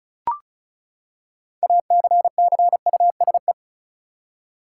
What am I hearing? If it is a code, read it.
Morse code, heard as a keyed tone, about 36 wpm, 700 Hz. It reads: ACCUSE
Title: Six Letter Words - 35wpm